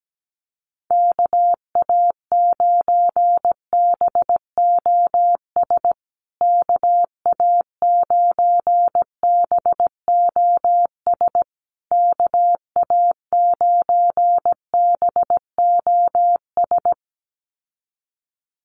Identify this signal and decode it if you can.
Morse code, heard as a keyed tone, about 17 wpm, 700 Hz. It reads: KA9BOS KA9BOS KA9BOS
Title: USA Call Signs Repeated 3x 17wpm